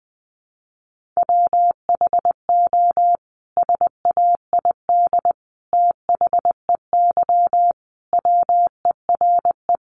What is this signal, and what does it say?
Morse code, heard as a keyed tone, about 20 wpm, 700 Hz. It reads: WHO SAID THEY WERE
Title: Sentences from Top 700 Words 20wpm